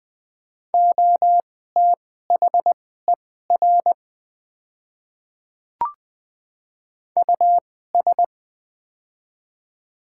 Morse code, heard as a keyed tone, about 20 wpm, spaced as for 14 wpm, 700 Hz. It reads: OTHER US
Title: 2x Character Spacing - Top 300 Words - 20wpm